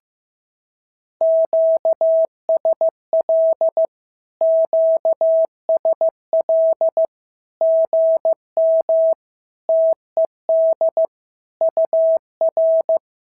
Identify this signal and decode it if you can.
Morse code, heard as a keyed tone, about 15 wpm, 650 Hz. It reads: QSL QSL GM TED UR
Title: SOTA QSO Practice 15wpm